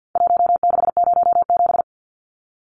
Morse code, heard as a keyed tone, about 49 wpm, 700 Hz. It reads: 1607
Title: Numbers - Contest Style - 4-Digit 50wpm